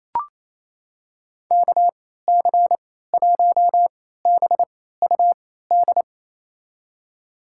Morse code, heard as a keyed tone, about 28 wpm, spaced as for 16 wpm, 700 Hz. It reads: KC1BUD
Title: USA Call Signs - 6 Characters - 3x Character Spacing - 28wpm